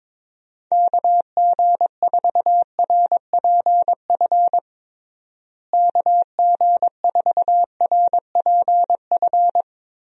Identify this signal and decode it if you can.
Morse code, heard as a keyed tone, about 22 wpm, 700 Hz. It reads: KG4RPF KG4RPF
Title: USA Call Signs - 6 Characters - Repeated 3x - 22wpm